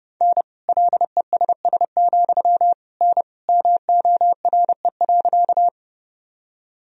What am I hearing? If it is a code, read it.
Morse code, heard as a keyed tone, about 30 wpm, 700 Hz. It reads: N LESS, N MORE.
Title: Sentences from Top 400 Words and Abbreviations 30wpm